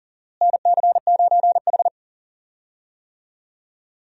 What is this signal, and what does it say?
Morse code, heard as a keyed tone, about 40 wpm, 700 Hz. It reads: NC9H
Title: USA Call Signs - 4 Characters - Repeated 3x - 40wpm